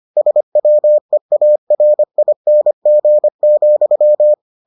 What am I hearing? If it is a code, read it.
Morse code, heard as a keyed tone, about 25 wpm, 600 Hz. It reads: SWEARING,